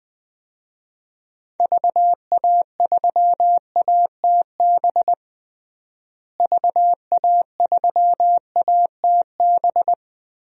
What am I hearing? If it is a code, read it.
Morse code, heard as a keyed tone, about 20 wpm, 700 Hz. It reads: VA3ATB VA3ATB
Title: DXCC-100 Call Signs - 6 Characters - Repeated 3x - 20wpm